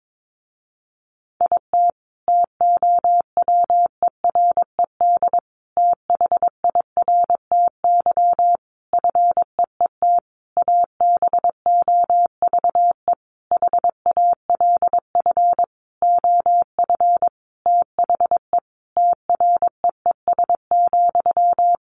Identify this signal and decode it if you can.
Morse code, heard as a keyed tone, about 22 wpm, 700 Hz. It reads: IT TOWERED THIRTY FEET ABOVE HALF OF THE TREES,
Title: A Sound Of Thunder 22wpm